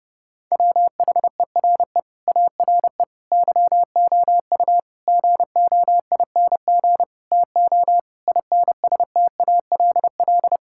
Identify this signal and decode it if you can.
Morse code, heard as a keyed tone, about 30 wpm, 700 Hz. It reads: WHERE ARE YOU GOING TO INSTALL
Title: Sentences from Top 1600 Words 30wpm